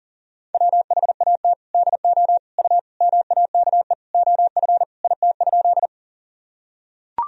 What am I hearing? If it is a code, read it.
Morse code, heard as a keyed tone, about 40 wpm, 700 Hz. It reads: WHAT DO U MAKE OF IT?
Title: Sentences from Top 100 Words and Abbreviations 40wpm